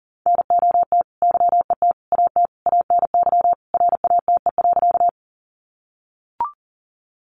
Morse code, heard as a keyed tone, about 40 wpm, 700 Hz. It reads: NOT YET AT ANY RATE.
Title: Sentences from Top 700 Words 40wpm